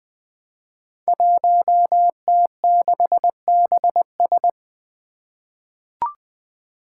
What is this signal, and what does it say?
Morse code, heard as a keyed tone, about 20 wpm, 700 Hz. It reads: 1T6BS
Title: Five Challenging Letter-Number Combinations 20wpm